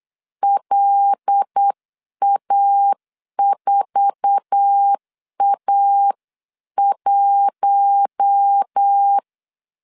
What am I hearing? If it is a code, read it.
Morse code, heard as a keyed tone, about 8 wpm, 800 Hz. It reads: LA4A1
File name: CW / DET / 1 9 1500C S A 8wpm 5637